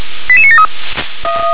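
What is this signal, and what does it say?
Five-tone selective call (ZVEI), heard as quick runs of five tones, tones of about 70 ms. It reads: 88973
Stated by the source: ZVEI five-tone call received on SDR.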